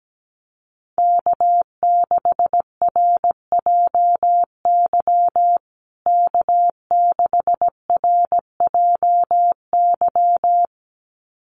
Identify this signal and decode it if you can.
Morse code, heard as a keyed tone, about 17 wpm, 700 Hz. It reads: K6RJY K6RJY
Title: USA Call Signs Repeated 2x 17wpm